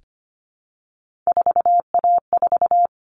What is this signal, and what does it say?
Morse code, heard as a keyed tone, about 25 wpm, 700 Hz. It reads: 4A4
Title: Common Cut Numbers - 3-Digit 25wpm